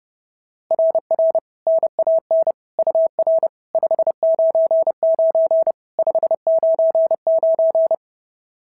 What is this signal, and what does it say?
Morse code, heard as a keyed tone, about 30 wpm, 660 Hz. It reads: RR NAN UR 599 599
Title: SOTA QSO Practice 30wpm